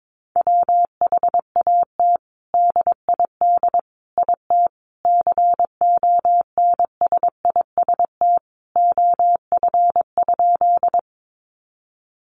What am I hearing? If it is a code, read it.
Morse code, heard as a keyed tone, about 22 wpm, 700 Hz. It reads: WHAT DID IT CONSIST OF?
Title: Sentences from Top 1400 Words 22wpm